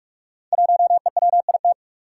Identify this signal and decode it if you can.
Morse code, heard as a keyed tone, about 44 wpm, 700 Hz. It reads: 1EWIT
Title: Five Challenging Letter-Number Combinations 45wpm